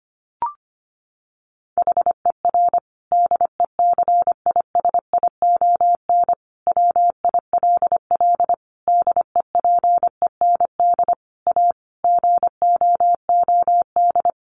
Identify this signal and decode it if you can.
Morse code, heard as a keyed tone, about 25 wpm, 700 Hz. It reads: HER DECISION WILL DEPEND A GOOD
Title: Sentences from Top 1300 Words 25wpm